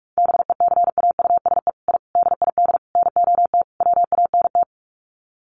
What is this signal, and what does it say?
Morse code, heard as a keyed tone, about 44 wpm, 700 Hz. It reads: BECAUSE I DID NOT WANT
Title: Sets of 5 Words Encore2 45wpm